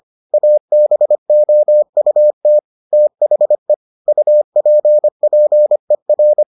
Morse code, heard as a keyed tone, about 25 wpm, 600 Hz. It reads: ABOUT THE UPPER